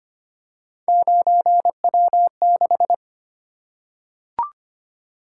Morse code, heard as a keyed tone, about 25 wpm, 700 Hz. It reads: 9W6
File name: Three Letter-Number Combinations - 25wpm